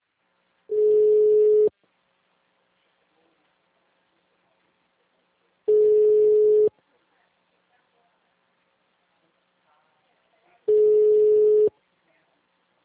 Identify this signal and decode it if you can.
Ringback tone (Europe, 425 Hz), with speech.